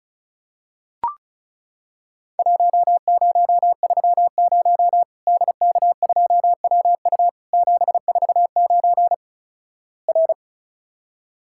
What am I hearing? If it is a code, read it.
Morse code, heard as a keyed tone, about 36 wpm, 700 Hz. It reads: 1030 DK2WU 749 R
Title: Worked All Europe DX Contest - QTC Exchange - 35wpm